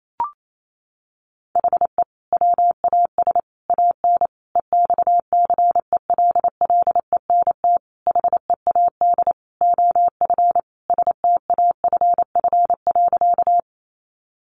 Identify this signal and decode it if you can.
Morse code, heard as a keyed tone, about 28 wpm, 700 Hz. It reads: HE WAS AN EXCELLENT HEAD OF STAFF.
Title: Sentences from Top 1200 Words 28wpm